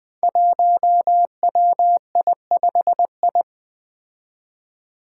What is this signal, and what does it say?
Morse code, heard as a keyed tone, about 20 wpm, 700 Hz. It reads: 1WI5I
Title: Five Challenging Letter-Number Combinations 20wpm